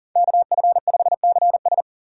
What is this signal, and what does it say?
Morse code, heard as a keyed tone, about 40 wpm, 700 Hz. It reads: KF5CS
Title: USA Call Signs - 5 Characters - 6x Character Spacing - 40wpm